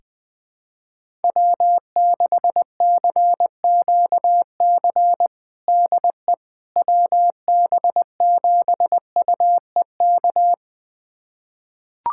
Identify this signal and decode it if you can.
Morse code, heard as a keyed tone, about 20 wpm, 700 Hz. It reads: W6CQC DE WB7UEK
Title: USA Callsign DE Callsign 20wpm